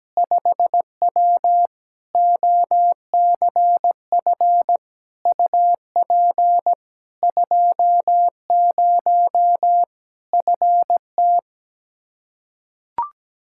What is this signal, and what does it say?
Morse code, heard as a keyed tone, about 17 wpm, 700 Hz. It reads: 5W OCF UP 20 FT
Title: Practice QSO - Exchange 2 - Info - 17wpm